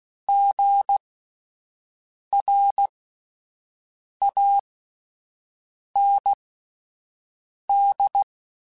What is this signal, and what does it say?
Morse code, heard as a keyed tone, about 16 wpm, spaced as for 5 wpm, 800 Hz. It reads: GRAND